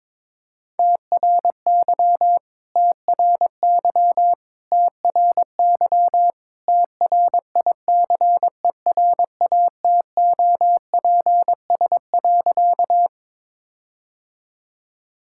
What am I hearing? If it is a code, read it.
Morse code, heard as a keyed tone, about 22 wpm, 700 Hz. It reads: TRY TRY TRY TRICERATOPS.
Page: morse-code-ninja-practice-22wpm